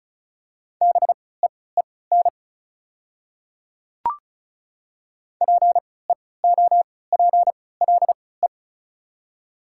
Morse code, heard as a keyed tone, about 34 wpm, spaced as for 20 wpm, 690 Hz. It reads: BEEN PEOPLE
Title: 3x Character Spacing - Top 100 Words - 35wpm